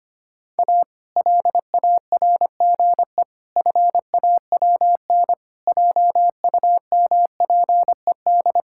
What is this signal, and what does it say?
Morse code, heard as a keyed tone, about 25 wpm, 700 Hz. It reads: A LARGE FAWN JUMPED
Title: Pangram Challenge v2 - 25wpm